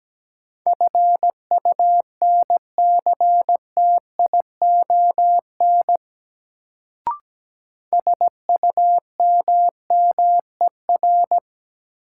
Morse code, heard as a keyed tone, about 17 wpm, 700 Hz. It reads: FUNCTION SUMMER
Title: Sentences from Top 1100 Words - New Word Drill Alt - 17wpm